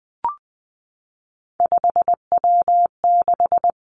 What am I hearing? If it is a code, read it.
Morse code, heard as a keyed tone, about 20 wpm, 700 Hz. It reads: 5W6
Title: Three Challenging Letter-Number Combinations 20wpm